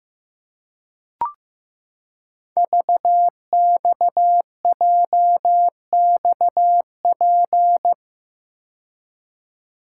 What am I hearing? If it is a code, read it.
Morse code, heard as a keyed tone, about 15 wpm, 700 Hz. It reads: VXJXP
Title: Code Groups - 5 Letter:Numbers - 15wpm